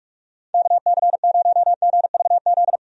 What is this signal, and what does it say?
Morse code, heard as a keyed tone, about 44 wpm, 700 Hz. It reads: KC0GVZ
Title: USA Call Signs - 6 Characters - 45wpm